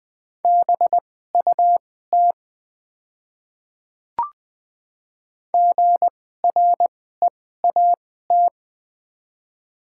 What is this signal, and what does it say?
Morse code, heard as a keyed tone, about 20 wpm, spaced as for 14 wpm, 690 Hz. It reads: BUT GREAT
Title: 2x Character Spacing - Top 300 Words - 20wpm